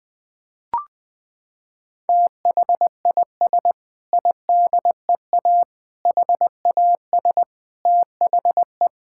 Morse code, heard as a keyed tone, about 20 wpm, 700 Hz. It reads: THIS IDEA HAS THE
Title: Sentences with 6-Word Phrases - 20wpm